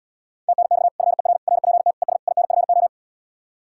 Morse code, heard as a keyed tone, about 77 wpm, 700 Hz. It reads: TNX FER RPT ES INFO